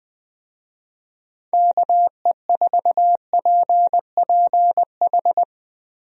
Morse code, heard as a keyed tone, about 20 wpm, 700 Hz. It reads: KE4PPH
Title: USA Call Signs - 6 Characters - Repeated 3x - 20wpm